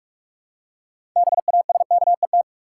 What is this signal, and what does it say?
Morse code, heard as a keyed tone, about 42 wpm, 700 Hz. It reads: BASKET
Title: Six Letter Words - 45wpm